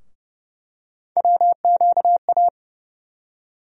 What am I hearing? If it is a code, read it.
Morse code, heard as a keyed tone, about 29 wpm, 700 Hz. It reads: WQA
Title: USA Call Sign Postfixes - 3 Letters 30wpm